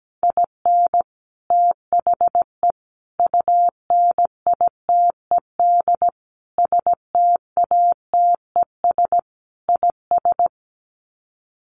Morse code, heard as a keyed tone, about 17 wpm, 700 Hz. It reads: IN THE UNITED STATES IS
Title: Sets of 5 Words 17wpm